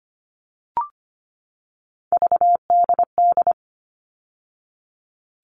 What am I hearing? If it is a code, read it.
Morse code, heard as a keyed tone, about 25 wpm, 700 Hz. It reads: VDD
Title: All Cut Numbers - 3-Digit 25wpm